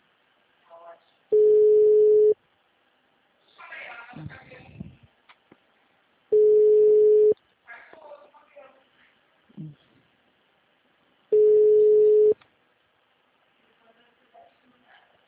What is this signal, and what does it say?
Ringback tone (Europe, 425 Hz), with speech.